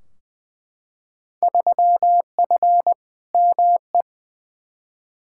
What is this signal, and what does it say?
Morse code, heard as a keyed tone, about 20 wpm, 700 Hz. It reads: 3F ME
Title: Field Day Exchanges - 20wpm